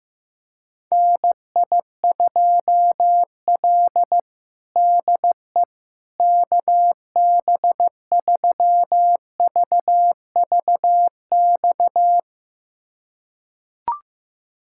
Morse code, heard as a keyed tone, about 15 wpm, 700 Hz. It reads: NI2L DE KB3VVX
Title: USA Callsign DE Callsign 15wpm